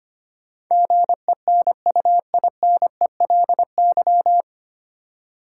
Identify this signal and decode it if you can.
Morse code, heard as a keyed tone, about 25 wpm, 700 Hz. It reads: GENUINELY
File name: Nine Letter Words - 25wpm